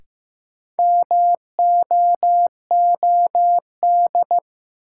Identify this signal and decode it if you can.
Morse code, heard as a keyed tone, about 15 wpm, 700 Hz. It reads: MOOD
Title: Sentences from Top 1100 Words - New Word Drill Alt - 15wpm